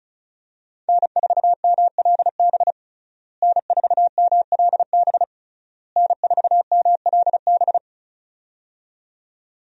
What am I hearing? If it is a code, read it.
Morse code, heard as a keyed tone, about 36 wpm, 700 Hz. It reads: N4MLB N4MLB N4MLB
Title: USA Call Signs - 5 Characters - Repeated 3x - 35wpm